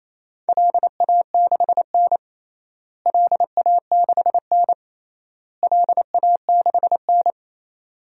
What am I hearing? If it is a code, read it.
Morse code, heard as a keyed tone, about 28 wpm, 700 Hz. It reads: LA6N LA6N LA6N
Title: DXCC-100 Call Signs - 4 Characters - Repeated 3x - 28wpm